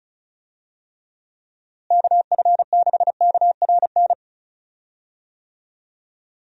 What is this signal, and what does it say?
Morse code, heard as a keyed tone, about 36 wpm, 700 Hz. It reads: KF6KRN
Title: USA Call Signs - 6 Characters - Repeated 2x - 35wpm